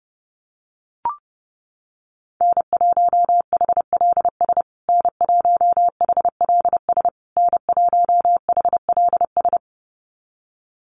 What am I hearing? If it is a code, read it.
Morse code, heard as a keyed tone, about 29 wpm, 700 Hz. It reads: N1HLS N1HLS N1HLS
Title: USA Call Signs Repeated 3x 30wpm